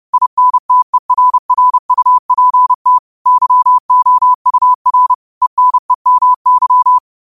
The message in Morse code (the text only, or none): INTERRUPT YOUR ENEMY